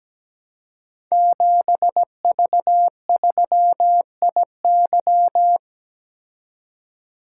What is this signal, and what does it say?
Morse code, heard as a keyed tone, about 17 wpm, 700 Hz. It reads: 7V3IY
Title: Code Groups - 5 Letter:Numbers - 17wpm